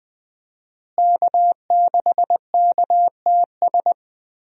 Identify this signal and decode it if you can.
Morse code, heard as a keyed tone, about 20 wpm, 700 Hz. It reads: K6KTS
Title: USA Call Signs - 5 Characters - Repeated 2x - 20wpm